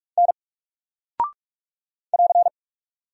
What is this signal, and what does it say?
Morse code, heard as a keyed tone, about 44 wpm, 700 Hz. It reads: N <AR>
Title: Single Character (Letter:Number:Punctuation:Prosign) Alt 45wpm